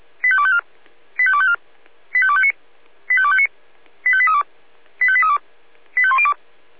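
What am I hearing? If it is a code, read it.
Five-tone selective call (ZVEI), heard as quick runs of five tones, tones of about 70 ms. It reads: 85254 85254 85259 85259 85832 85832 85102
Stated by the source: ZVEI five-tone selective call received on an SDR.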